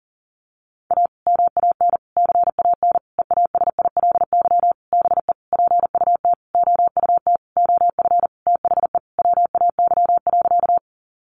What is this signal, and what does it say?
Morse code, heard as a keyed tone, about 40 wpm, 700 Hz. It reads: A MAN CAN EASILY BE PUT OUT OF THE WAY.
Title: Sentences from Top 900 Words 40wpm